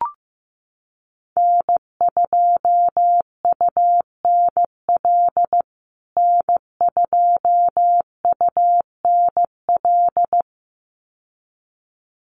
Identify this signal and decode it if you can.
Morse code, heard as a keyed tone, about 15 wpm, 700 Hz. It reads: N2UNL N2UNL
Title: USA Call Signs Repeated 2x 15wpm